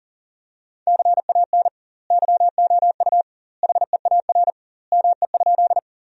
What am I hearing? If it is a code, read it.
Morse code, heard as a keyed tone, about 40 wpm, 700 Hz. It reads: CAN YOU HEAR ME?
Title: 2x Word-Spacing - Sentences from Top 300 Words - 40wpm